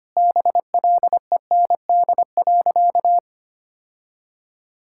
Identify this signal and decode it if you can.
Morse code, heard as a keyed tone, about 25 wpm, 700 Hz. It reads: BLEND.
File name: practice / 25wpm / Sentences with 4-Word Phrases - 25wpm